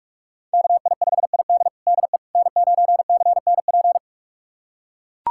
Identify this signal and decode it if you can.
Morse code, heard as a keyed tone, about 41 wpm, 700 Hz. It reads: KI5ID DE N9CNP
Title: USA Callsign DE Callsign - 5 Characters - 45wpm